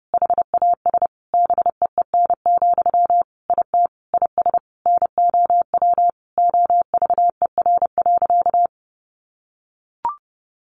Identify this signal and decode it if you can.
Morse code, heard as a keyed tone, about 30 wpm, 700 Hz. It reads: HAS BEEN, IT IS NOW OVER.
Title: Sentences from Top 1100 Words 30wpm